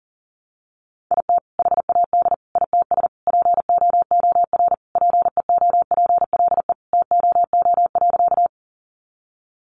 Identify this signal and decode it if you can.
Morse code, heard as a keyed tone, about 37 wpm, 700 Hz. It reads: IT HAD ITS POOR PEOPLE TOO.
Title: Sentences from Top 1000 Words 40wpm